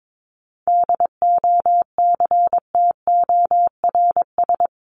Morse code, heard as a keyed tone, about 22 wpm, 700 Hz. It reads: DOCTORS
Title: Easy Phrases 22wpm